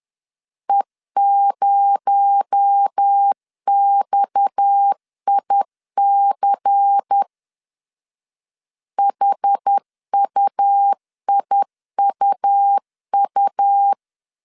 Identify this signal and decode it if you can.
Morse code, heard as a keyed tone, about 10 wpm, 800 Hz. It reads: E0XIC HUIUU
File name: CW / DET / 1 6 1500C S B 10wpm 4137